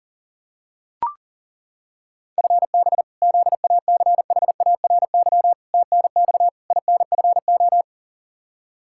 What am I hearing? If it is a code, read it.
Morse code, heard as a keyed tone, about 37 wpm, 700 Hz. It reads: FB ZACHARY TNX INFO
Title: Practice QSO - Exchange 3 - More Info - 40wpm